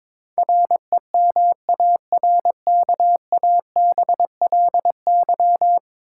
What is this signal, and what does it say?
Morse code, heard as a keyed tone, about 22 wpm, 700 Hz. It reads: REMARKABLY
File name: practice / 22wpm / 10 Letter Words - First 3 Letters Given - 22wpm